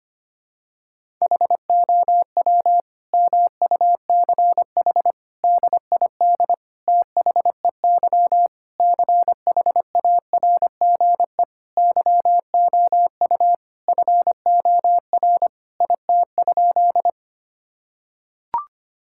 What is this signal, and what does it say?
Morse code, heard as a keyed tone, about 25 wpm, 700 Hz. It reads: HOW MUCH DID THEY CHARGE YOU FOR IT?
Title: Sentences from Top 1000 Words - Review Part 6 - 25wpm